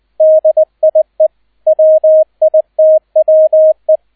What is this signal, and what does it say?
Morse code, heard as a keyed tone, about 20 wpm, 630 Hz, with radio noise. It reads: DIE WITWE